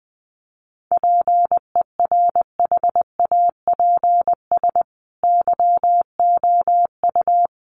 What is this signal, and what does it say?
Morse code, heard as a keyed tone, about 20 wpm, 700 Hz. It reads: PERHAPS YOU
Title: Sentences from Top 1400 Words 20wpm